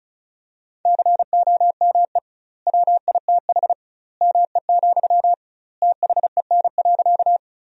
Morse code, heard as a keyed tone, about 36 wpm, 700 Hz. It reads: COME WITH ME, THEN.
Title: 2x Word-Spacing - Sentences from Top 200 Words - 35wpm